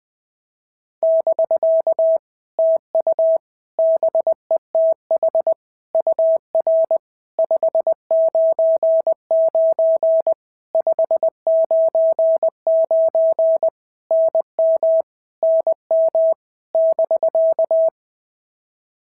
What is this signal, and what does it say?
Morse code, heard as a keyed tone, about 19 wpm, 650 Hz. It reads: <BK> TU BETH UR 599 599 NM NM <BK>
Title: POTA QSO Practice - 20wpm